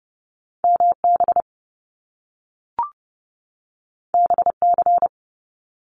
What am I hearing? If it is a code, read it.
Morse code, heard as a keyed tone, about 30 wpm, 700 Hz. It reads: MB BC
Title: Canadian Provinces and Territories 30wpm